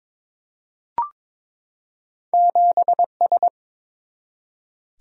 Morse code, heard as a keyed tone, about 22 wpm, 700 Hz. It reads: 7S
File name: practice / 22wpm / Two Letter-Number Combinations - 22wpm